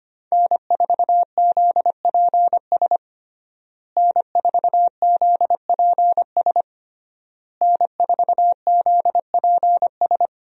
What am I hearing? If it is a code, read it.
Morse code, heard as a keyed tone, about 25 wpm, 700 Hz. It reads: N4ZPS N4ZPS N4ZPS
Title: USA Call Signs - 5 Characters - Repeated 3x - 25wpm